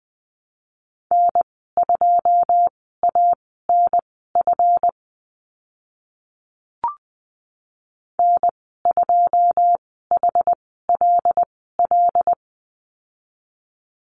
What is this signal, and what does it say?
Morse code, heard as a keyed tone, about 20 wpm, spaced as for 15 wpm, 700 Hz. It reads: N2ANF N2HLL
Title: 140 Lesson 23g - Letter f - Call Signs 20wpm